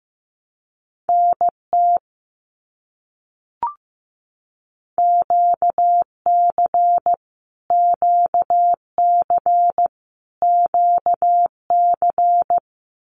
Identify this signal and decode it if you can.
Morse code, heard as a keyed tone, about 15 wpm, 700 Hz. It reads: NT QC QC QC
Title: Canadian Provinces and Territories - Repeated 3x - 15wpm